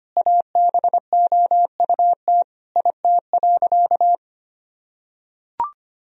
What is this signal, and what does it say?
Morse code, heard as a keyed tone, about 25 wpm, 700 Hz. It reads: ABOUT IT.
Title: Sentences from Top 1000 Words - Review Part 2 - 25wpm